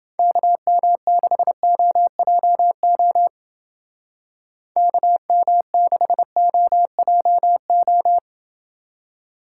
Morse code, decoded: KM6OJO KM6OJO

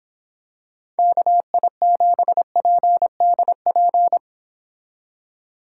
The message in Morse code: KI7PDP

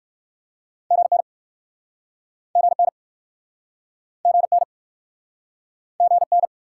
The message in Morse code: GN GN GN GN

